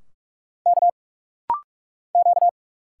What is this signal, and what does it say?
Morse code, heard as a keyed tone, about 46 wpm, 700 Hz. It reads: K Q
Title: Single Letters - Rapid-Fire 45wpm